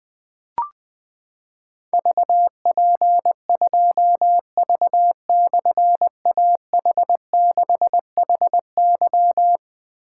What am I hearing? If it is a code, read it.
Morse code, heard as a keyed tone, about 20 wpm, 700 Hz. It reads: VP2V/AH6HY
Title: Difficult Call Signs 20wpm